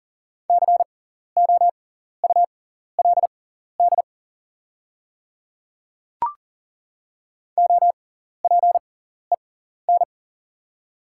Morse code, heard as a keyed tone, about 40 wpm, spaced as for 14 wpm, 690 Hz. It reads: COULD OPEN
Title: 6x Character Spacing - Top 500 Words - 40wpm